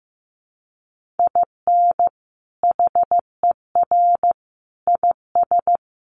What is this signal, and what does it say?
Morse code, heard as a keyed tone, about 15 wpm, 700 Hz. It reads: IN HER IS
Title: Sentences from Top 1400 Words 15wpm